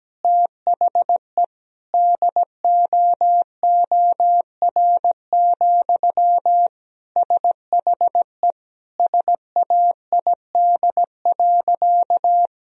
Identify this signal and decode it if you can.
Morse code, heard as a keyed tone, about 17 wpm, 700 Hz. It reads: THE DOOR, SHE SAID.
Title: Sentences from Top 1000 Words - Review Part 1 - 17wpm